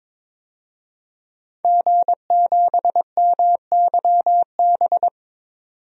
Morse code, heard as a keyed tone, about 22 wpm, 700 Hz. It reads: G7MYB
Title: DXCC-100 Call Signs - 5 Characters - 22wpm